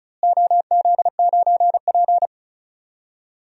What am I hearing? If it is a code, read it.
Morse code, heard as a keyed tone, about 36 wpm, 700 Hz. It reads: OZ9P
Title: DXCC-100 Call Signs - 4 Characters - Repeated 3x - 35wpm